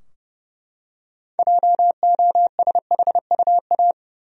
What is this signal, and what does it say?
Morse code, heard as a keyed tone, about 30 wpm, 700 Hz. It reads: JOSHUA
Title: USA Common Names - 30wpm